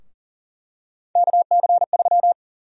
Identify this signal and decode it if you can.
Morse code, heard as a keyed tone, about 37 wpm, 700 Hz. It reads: KC3
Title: USA Call Sign Prefixes - Repeated 2x - 40wpm